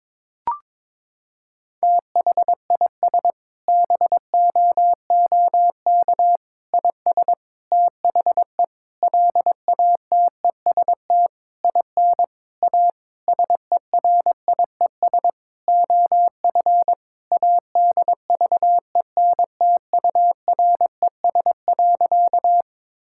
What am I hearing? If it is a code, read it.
Morse code, heard as a keyed tone, about 22 wpm, 700 Hz. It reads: THIS BOOK IS THE LATEST IN A SERIES OF ADVENTURES.
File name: Sentences with 7-Word Phrases - 22wpm